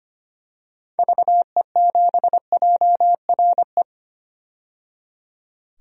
Morse code, heard as a keyed tone, about 25 wpm, 700 Hz. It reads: VE7JRE
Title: DXCC-100 Call Signs - 6 Characters - 25wpm